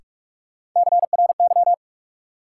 Morse code, heard as a keyed tone, about 46 wpm, 700 Hz. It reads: CRY
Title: Sentences from Top 500 Words - New Word Drill - 45wpm